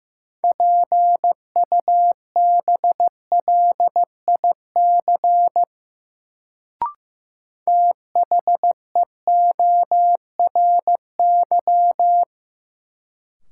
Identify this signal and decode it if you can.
Morse code, heard as a keyed tone, about 15 wpm, 700 Hz. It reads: PUBLIC THEORY
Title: Sentences from Top 600 Words - New Word Drill - 15wpm